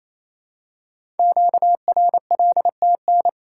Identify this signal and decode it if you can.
Morse code, heard as a keyed tone, about 28 wpm, 700 Hz. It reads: QRLTN
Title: Code Groups - 5 Letters - 28wpm